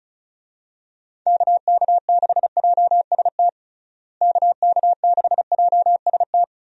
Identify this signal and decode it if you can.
Morse code, heard as a keyed tone, about 34 wpm, 700 Hz. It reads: KK6JST KK6JST
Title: USA Call Signs - 6 Characters - Repeated 2x - 35wpm